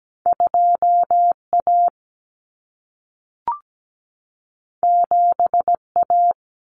Morse code, heard as a keyed tone, about 17 wpm, 700 Hz. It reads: 2A 7A
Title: Common Cut Numbers - 2-Digit 17wpm